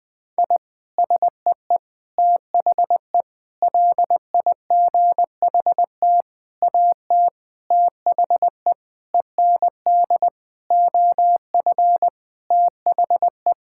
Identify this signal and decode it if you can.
Morse code, heard as a keyed tone, about 19 wpm, 700 Hz. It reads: I SEE THE LIGHT AT THE END OF THE
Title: Sentences with 8-Word Phrases - 20wpm